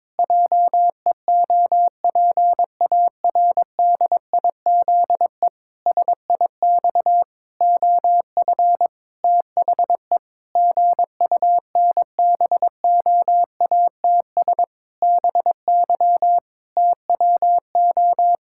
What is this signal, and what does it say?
Morse code, heard as a keyed tone, about 22 wpm, 700 Hz. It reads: JEOPARDIZE SIX OF THE GUNBOATS BY TWO
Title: Pangram Challenge v2 - 22wpm